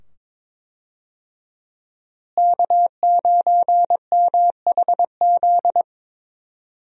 Morse code, heard as a keyed tone, about 22 wpm, 700 Hz. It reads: K9MHZ
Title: USA Call Signs - 5 Characters - Repeated 2x - 22wpm